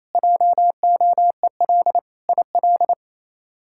Morse code, heard as a keyed tone, about 28 wpm, 700 Hz. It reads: JOEL IL
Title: K1USN SST QSO Practice - 28wpm